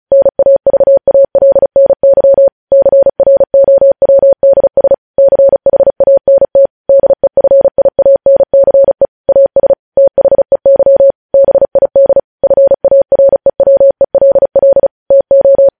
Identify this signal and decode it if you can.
Morse code, heard as a keyed tone, about 36 wpm, 550 Hz. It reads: NAVALNY CROWDS CHANT DEFIANCE AS THEY BID FAREWELL TO
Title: News Headlines at 35